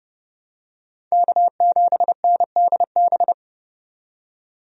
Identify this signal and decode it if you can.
Morse code, heard as a keyed tone, about 31 wpm, 700 Hz. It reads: K7NDB
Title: USA Call Signs - 5 Characters - 30wpm